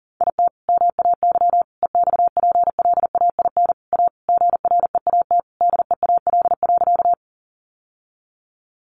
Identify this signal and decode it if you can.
Morse code, heard as a keyed tone, about 40 wpm, 700 Hz. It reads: IT MAY EXPLAIN A GREAT DEAL.
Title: Sentences from Top 1100 Words 40wpm